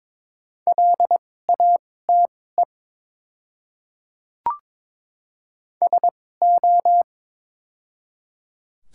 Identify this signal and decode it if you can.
Morse code, heard as a keyed tone, about 22 wpm, spaced as for 16 wpm, 700 Hz. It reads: LATE SO
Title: 2x Character Spacing - Top 300 Words - 22wpm